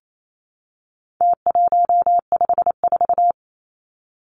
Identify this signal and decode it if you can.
Morse code, heard as a keyed tone, about 28 wpm, 700 Hz. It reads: T154
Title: Numbers - Contest Style - 4-Digit 28wpm